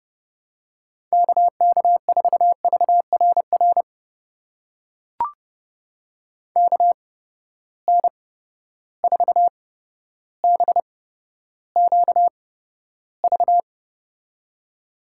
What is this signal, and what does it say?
Morse code, heard as a keyed tone, about 30 wpm, spaced as for 13 wpm, 700 Hz. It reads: KK4VRR KN4BQV
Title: USA Call Signs - 6 Characters - 8x Character Spacing - 30wpm